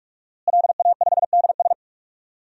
Morse code, heard as a keyed tone, about 46 wpm, 700 Hz. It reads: LA5DS
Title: DXCC-100 Call Signs - 5 Characters - Repeated 3x - 45wpm